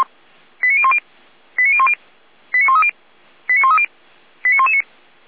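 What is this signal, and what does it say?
Five-tone selective call (ZVEI), heard as quick runs of five tones, tones of about 70 ms. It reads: 89010 89010 89130 89130 89119
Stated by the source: ZVEI five-tone call 89017 received by SDR.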